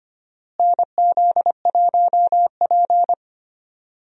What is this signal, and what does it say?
Morse code, heard as a keyed tone, about 25 wpm, 700 Hz. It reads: NZ1P
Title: USA Call Signs - 4 Characters - Repeated 2x - 25wpm